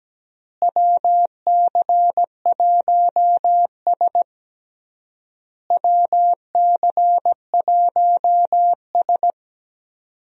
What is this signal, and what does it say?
Morse code, heard as a keyed tone, about 17 wpm, 700 Hz. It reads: WC1S WC1S
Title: USA Call Signs - 4 Characters - Repeated 3x - 17wpm